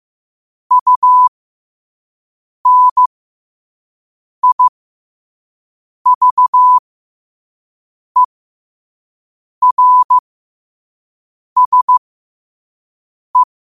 Morse code, heard as a keyed tone, about 15 wpm, spaced as for 5 wpm, 1000 Hz. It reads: UNIVERSE
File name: 5wpm